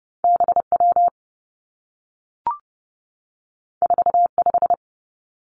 Morse code, heard as a keyed tone, about 30 wpm, 700 Hz. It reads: B W 4 5
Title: Two Challenging Letter-Number Combinations 30wpm